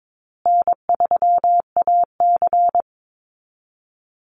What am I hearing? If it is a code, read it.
Morse code, heard as a keyed tone, about 22 wpm, 700 Hz. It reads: N3AC
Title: Easy Call Signs Repeated 3x - 22wpm